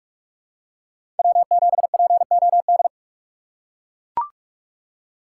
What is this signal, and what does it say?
Morse code, heard as a keyed tone, about 44 wpm, 690 Hz. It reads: W7POD
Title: USA Call Signs - 5 Characters - 2x Character Spacing - 45wpm